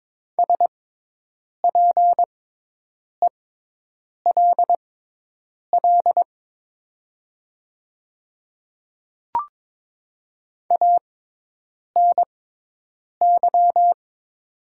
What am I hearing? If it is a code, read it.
Morse code, heard as a keyed tone, about 22 wpm, spaced as for 8 wpm, 700 Hz. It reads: SPELL ANY
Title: 6x Character Spacing - Top 200 Words - 22wpm